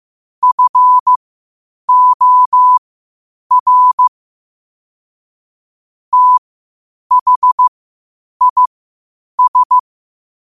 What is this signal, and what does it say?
Morse code, heard as a keyed tone, about 15 wpm, spaced as for 8 wpm, 1000 Hz. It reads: FOR THIS
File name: qotdfiles / 8wpm